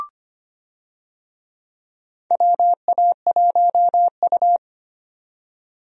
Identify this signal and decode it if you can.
Morse code, heard as a keyed tone, about 25 wpm, 700 Hz. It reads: WA1U
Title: USA Call Signs - 4 Characters - 25wpm